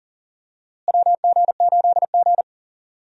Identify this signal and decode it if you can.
Morse code, heard as a keyed tone, about 40 wpm, 700 Hz. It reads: WG8G
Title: USA Call Signs - 4 Characters - Repeated 2x - 40wpm